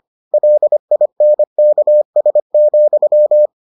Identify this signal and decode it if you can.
Morse code, heard as a keyed tone, about 25 wpm, 600 Hz. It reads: LINKS,